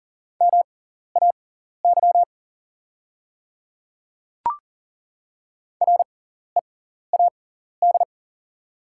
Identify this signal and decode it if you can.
Morse code, heard as a keyed tone, about 40 wpm, spaced as for 14 wpm, 700 Hz. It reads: MAY READ
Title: 6x Character Spacing - Top 300 Words - 40wpm